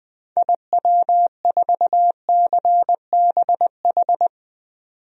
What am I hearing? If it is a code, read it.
Morse code, heard as a keyed tone, about 21 wpm, 700 Hz. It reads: IW4CBH
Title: DXCC-100 Call Signs - 6 Characters - 20wpm